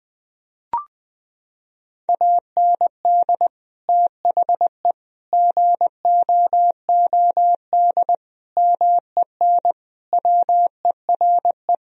Morse code, heard as a keyed tone, about 20 wpm, 700 Hz. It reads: AND THE GOOD MEN WERE
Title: Sentences from Top 1000 Words - Review Part 2 - 20wpm